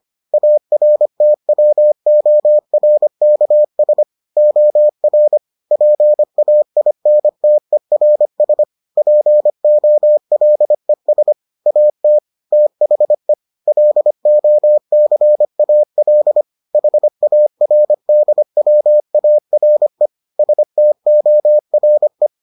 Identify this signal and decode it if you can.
Morse code, heard as a keyed tone, about 25 wpm, 600 Hz. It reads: ARTWORKS OR PAINTERS POLES AT THE LOCAL HARDWARE STORE